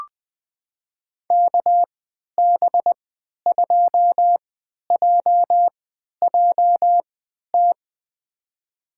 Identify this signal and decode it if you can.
Morse code, heard as a keyed tone, about 20 wpm, spaced as for 11 wpm, 700 Hz. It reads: KB2JJT
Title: USA Call Signs - 6 Characters - 3x Character Spacing - 20wpm